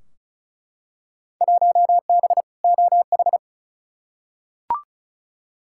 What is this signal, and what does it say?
Morse code, heard as a keyed tone, about 36 wpm, 700 Hz. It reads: 1B OH
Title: Field Day Exchanges - 35wpm